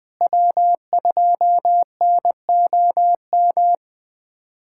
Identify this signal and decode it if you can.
Morse code, heard as a keyed tone, about 20 wpm, 700 Hz. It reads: W2NOM
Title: USA Callsign DE Callsign - 5 Characters - 20wpm